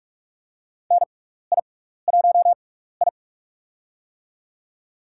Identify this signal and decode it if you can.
Morse code, heard as a keyed tone, about 44 wpm, spaced as for 15 wpm, 700 Hz. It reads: NI1I
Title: USA Call Signs - 4 Characters - 6x Character Spacing - 45wpm